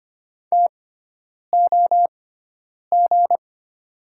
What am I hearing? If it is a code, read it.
Morse code, heard as a keyed tone, about 25 wpm, spaced as for 9 wpm, 700 Hz. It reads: TOG
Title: 6x Character Spacing - Top 500 Words - 25wpm